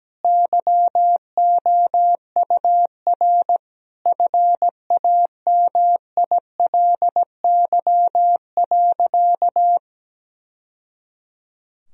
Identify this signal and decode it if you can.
Morse code, heard as a keyed tone, about 17 wpm, 700 Hz. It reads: YOUR FAMILY.
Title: Sentences from Top 1000 Words - Review Part 4 - 17wpm